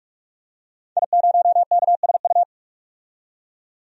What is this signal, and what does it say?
Morse code, heard as a keyed tone, about 46 wpm, 700 Hz. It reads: I0KSU
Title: DXCC-100 Call Signs - 5 Characters - 45wpm